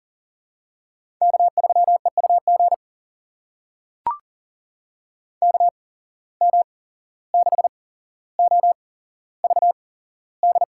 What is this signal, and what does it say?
Morse code, heard as a keyed tone, about 42 wpm, spaced as for 16 wpm, 700 Hz. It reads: K3EUG KM6OVD
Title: USA Call Signs - 8x Character Spacing - 40wpm